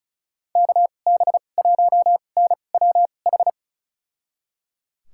Morse code, heard as a keyed tone, about 36 wpm, spaced as for 25 wpm, 690 Hz. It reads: KB1NWH
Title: USA Call Signs - 6 Characters - 2x Character Spacing - 35wpm